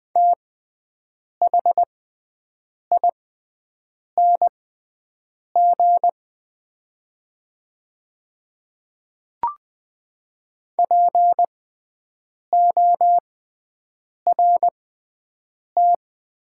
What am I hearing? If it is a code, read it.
Morse code, heard as a keyed tone, about 20 wpm, spaced as for 7 wpm, 700 Hz. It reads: THING PORT
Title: 6x Character Spacing - Top 200 Words - 20wpm